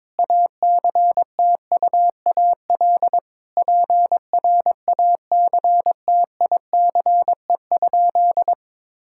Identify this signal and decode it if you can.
Morse code, heard as a keyed tone, about 22 wpm, 700 Hz. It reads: ACTUAL PRACTICE?
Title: Sentences from Top 1000 Words - Review Part 5 - 22wpm